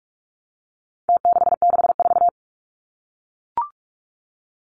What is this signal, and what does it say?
Morse code, heard as a keyed tone, about 44 wpm, 700 Hz. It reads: T664
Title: Numbers - Contest Style - 4-Digit 45wpm